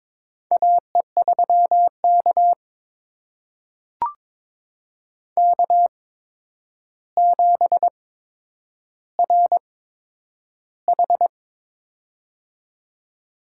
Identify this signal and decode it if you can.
Morse code, heard as a keyed tone, about 22 wpm, spaced as for 9 wpm, 700 Hz. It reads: AE3K K7RH
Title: USA Call Signs - 4 Characters - 8x Character Spacing - 22wpm